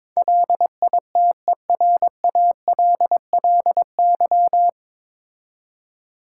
Morse code, heard as a keyed tone, about 21 wpm, 700 Hz. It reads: LITERALLY
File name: 9 Letter Words - First 3 Letters Given - 22wpm